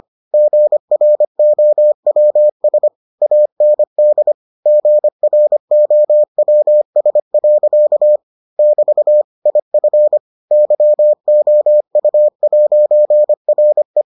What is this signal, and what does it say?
Morse code, heard as a keyed tone, about 25 wpm, 600 Hz. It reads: GROWS AND GROWS. <BT> IF YOU'RE